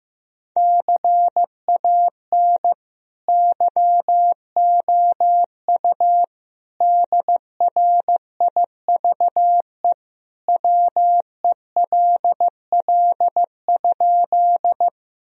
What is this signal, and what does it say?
Morse code, heard as a keyed tone, about 16 wpm, 700 Hz. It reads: CAN YOU DRIVE WELL?
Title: Sentences from Top 1000 Words - Review Part 2 - 15wpm